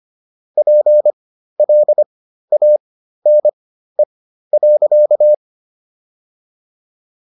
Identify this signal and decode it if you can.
Morse code, heard as a keyed tone, about 25 wpm, spaced as for 13 wpm, 600 Hz. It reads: PLANE.